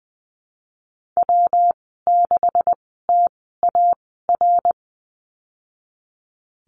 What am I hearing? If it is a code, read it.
Morse code, heard as a keyed tone, about 19 wpm, spaced as for 14 wpm, 700 Hz. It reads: W6TAR
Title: 154 Lesson 25g - Letter p - Call Signs 20wpm